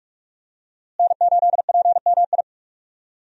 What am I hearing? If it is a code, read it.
Morse code, heard as a keyed tone, about 46 wpm, 700 Hz. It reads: N8PMI
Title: USA Call Signs - 5 Characters - Repeated 3x - 45wpm